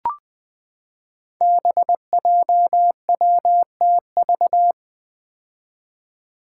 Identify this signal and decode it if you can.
Morse code, heard as a keyed tone, about 20 wpm, 700 Hz. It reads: BJWTV
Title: Five Challenging Letter-Number Combinations 20wpm